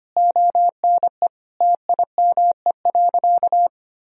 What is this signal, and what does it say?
Morse code, heard as a keyed tone, about 25 wpm, 700 Hz. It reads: ONE TIME.
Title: Sentences from Top 1000 Words - Review Part 2 - 25wpm